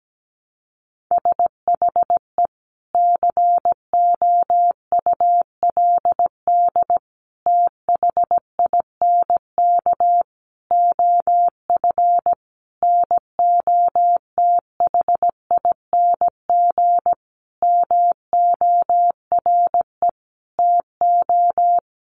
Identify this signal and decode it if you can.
Morse code, heard as a keyed tone, about 17 wpm, 700 Hz. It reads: SHE COULD THINK OF NOTHING MORE TO
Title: Sentences from Top 500 Words 17wpm